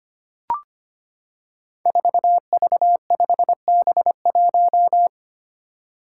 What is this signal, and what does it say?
Morse code, heard as a keyed tone, about 25 wpm, 700 Hz. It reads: 4V5B1
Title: Five Challenging Letter-Number Combinations 25wpm